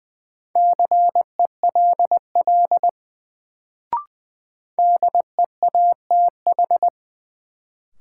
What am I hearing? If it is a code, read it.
Morse code, heard as a keyed tone, about 20 wpm, 700 Hz. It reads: CELL DEATH
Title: Sentences from Top 1100 Words - New Word Drill Alt - 20wpm